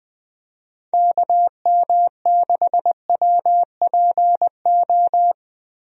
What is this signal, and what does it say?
Morse code, heard as a keyed tone, about 20 wpm, 700 Hz. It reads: KM6WPO
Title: USA Call Signs - 6 Characters - Repeated 2x - 20wpm